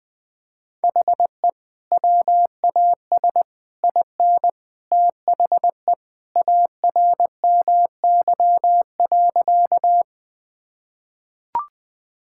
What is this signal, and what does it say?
Morse code, heard as a keyed tone, about 21 wpm, 700 Hz. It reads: HE WAS IN THE ARMY.
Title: Sentences with 3-Word Phrases - 20wpm